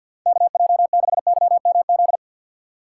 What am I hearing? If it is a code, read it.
Morse code, heard as a keyed tone, about 52 wpm, 690 Hz. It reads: KJ6YMZ
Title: USA Call Signs - 6 Characters - 3x Character Spacing - 50wpm